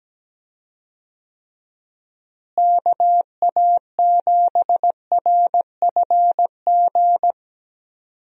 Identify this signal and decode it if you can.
Morse code, heard as a keyed tone, about 17 wpm, 700 Hz. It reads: KA7RFG
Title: USA Call Signs - 6 Characters - Repeated 2x - 17wpm